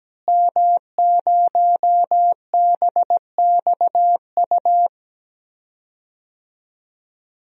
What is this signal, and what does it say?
Morse code, heard as a keyed tone, about 17 wpm, 700 Hz. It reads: M0BXU
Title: DXCC-100 Call Signs - 5 Characters - Repeated 3x - 17wpm